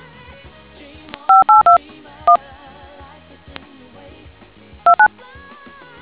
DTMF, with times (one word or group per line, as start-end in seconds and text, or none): 1.3-1.8 582
2.3-2.4 1
4.8-5.1 29